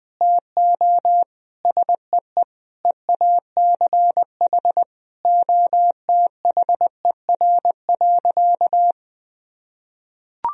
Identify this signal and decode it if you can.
Morse code, heard as a keyed tone, about 20 wpm, 700 Hz. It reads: TO SEE EACH OTHER.
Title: Sentences from Top 1000 Words - Review Part 1 - 20wpm